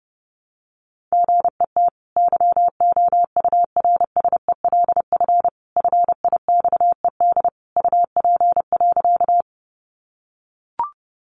Morse code, heard as a keyed tone, about 28 wpm, 700 Hz. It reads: GET YOURSELF FIXED UP.
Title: Sentences from Top 1100 Words 30wpm